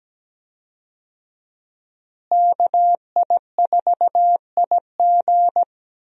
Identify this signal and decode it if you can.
Morse code, heard as a keyed tone, about 17 wpm, 700 Hz. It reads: KI4IG
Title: USA Call Signs - 5 Characters - Repeated 3x - 17wpm